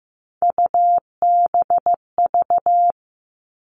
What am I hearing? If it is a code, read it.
Morse code, heard as a keyed tone, about 15 wpm, 700 Hz. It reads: UBV
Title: All Cut Numbers - 3-Digit 15wpm